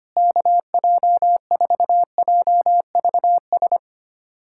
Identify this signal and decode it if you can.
Morse code, heard as a keyed tone, about 25 wpm, 700 Hz. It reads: KJ4JVS
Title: USA Callsign DE Callsign 25wpm